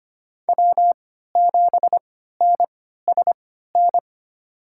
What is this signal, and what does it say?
Morse code, heard as a keyed tone, about 25 wpm, spaced as for 14 wpm, 700 Hz. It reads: W7NSN
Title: USA Call Signs - 5 Characters - 3x Character Spacing - 25wpm